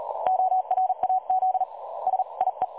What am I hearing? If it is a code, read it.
Morse code, heard as a keyed tone, about 38 wpm, 750 Hz. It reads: ORAZ SIE
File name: dlazaawansowanych / nagraniazpasma / websdr recording 3565.0kHz